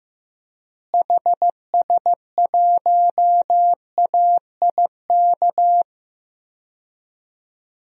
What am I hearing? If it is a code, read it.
Morse code, heard as a keyed tone, about 15 wpm, 700 Hz. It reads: HS1AIK
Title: DXCC-100 Call Signs - 6 Characters - Repeated 2x - 15wpm